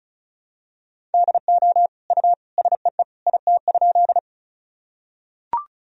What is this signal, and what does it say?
Morse code, heard as a keyed tone, about 34 wpm, 700 Hz. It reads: DO U SEE IT?
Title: Sentences from Top 100 Words and Abbreviations 35wpm